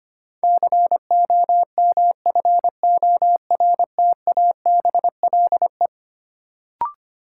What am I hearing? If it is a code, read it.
Morse code, heard as a keyed tone, about 25 wpm, 700 Hz. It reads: COMFORTABLE
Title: Sentences from Top 1000 Words - Review Part 7 - New Word Drill - 25wpm